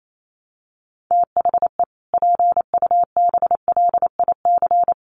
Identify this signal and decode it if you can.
Morse code, heard as a keyed tone, about 28 wpm, 700 Hz. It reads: THE PUBLIC
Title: Sets of 2 Words 28wpm